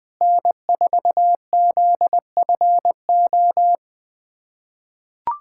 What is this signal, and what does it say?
Morse code, heard as a keyed tone, about 20 wpm, 700 Hz. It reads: N4ZFO
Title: USA Callsign DE Callsign - 5 Characters - 20wpm